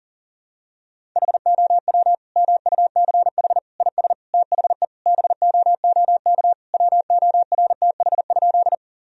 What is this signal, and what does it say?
Morse code, heard as a keyed tone, about 37 wpm, 700 Hz. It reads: HOW MUCH IS THE BOOK WORTH?
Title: Sentences from Top 1000 Words - Review Part 1 - 40wpm